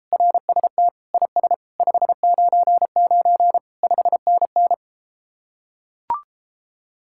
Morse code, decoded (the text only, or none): RST IS 599 5NN